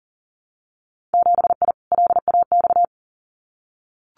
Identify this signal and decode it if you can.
Morse code, heard as a keyed tone, about 40 wpm, 700 Hz. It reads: 7I LAX
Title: Winter Field Day Exchanges Encore 40wpm